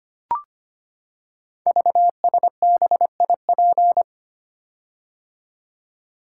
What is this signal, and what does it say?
Morse code, heard as a keyed tone, about 25 wpm, 700 Hz. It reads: VSBIP
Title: Code Groups - 5 Letter:Numbers - 25wpm